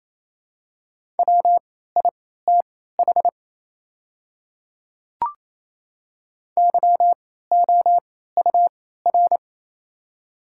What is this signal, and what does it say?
Morse code, heard as a keyed tone, about 28 wpm, spaced as for 16 wpm, 690 Hz. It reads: WITH YOUR
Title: 3x Character Spacing - Top 100 Words - 28wpm